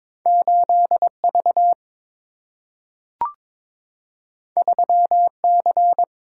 Morse code, heard as a keyed tone, about 22 wpm, 700 Hz. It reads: 8V 3C
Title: Single-Letter Single-Number 22wpm